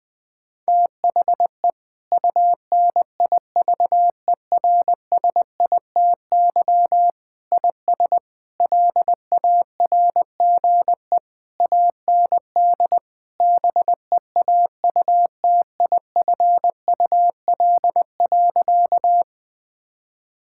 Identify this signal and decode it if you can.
Morse code, heard as a keyed tone, about 20 wpm, 700 Hz. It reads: THE UNIVERSITY IS LARGE AND BEAUTIFUL.
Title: Sentences from Top 1000 Words - Review Part 5 - 20wpm